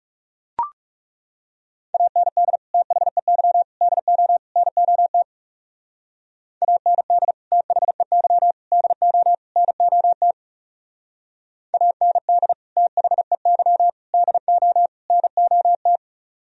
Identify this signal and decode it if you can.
Morse code, heard as a keyed tone, about 37 wpm, 700 Hz. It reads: AND THEY DO NOT AND THEY DO NOT AND THEY DO NOT
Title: Speed-Racing - Sets of 4 Words 30wpm